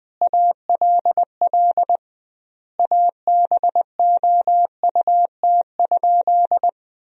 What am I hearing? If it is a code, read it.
Morse code, heard as a keyed tone, about 20 wpm, 700 Hz. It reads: ALL ABOUT?
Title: 2x Word-Spacing - Sentences from Top 100 Words - 20wpm